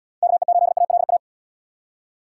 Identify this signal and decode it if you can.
Morse code, heard as a keyed tone, about 79 wpm, 700 Hz. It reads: KE8ALA